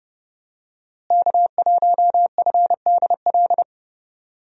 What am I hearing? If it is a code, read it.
Morse code, heard as a keyed tone, about 30 wpm, 700 Hz. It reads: K1FDL
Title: USA Call Signs - 5 Characters - Repeated 2x - 30wpm